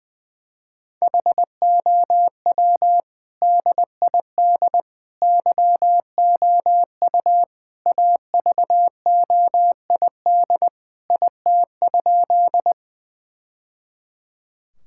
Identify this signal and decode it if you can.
Morse code, heard as a keyed tone, about 21 wpm, 700 Hz. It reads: HOW DID YOU AVOID IT?
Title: Sentences from Top 1000 Words - Review Part 6 - 20wpm